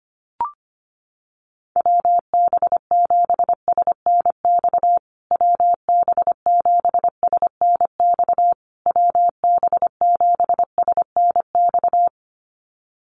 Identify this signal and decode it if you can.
Morse code, heard as a keyed tone, about 25 wpm, 700 Hz. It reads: WB7SNX WB7SNX WB7SNX
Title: USA Call Signs Repeated 3x 25wpm